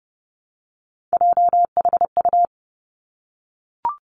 Morse code, heard as a keyed tone, about 30 wpm, 700 Hz. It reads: JHU
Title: Three Challenging Letter-Number Combinations 30wpm